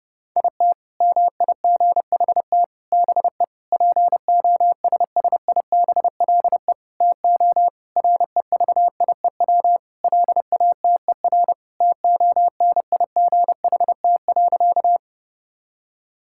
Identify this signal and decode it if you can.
Morse code, heard as a keyed tone, about 30 wpm, 700 Hz. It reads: IT MIGHT BE POSSIBLE TO REVIEW LATER TONIGHT.
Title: Sentences from Top 1000 Words - Review Part 6 - 30wpm